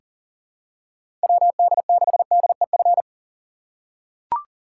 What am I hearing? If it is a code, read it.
Morse code, heard as a keyed tone, about 40 wpm, 700 Hz. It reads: WD6DEF
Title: USA Call Signs - 6 Characters - 3x Character Spacing - 40wpm